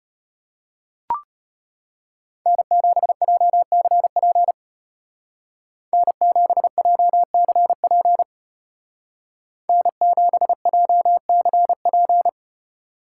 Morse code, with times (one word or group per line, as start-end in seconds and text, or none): 2.5-4.5 N7JCP
5.9-8.2 N7JCP
9.7-12.3 N7JCP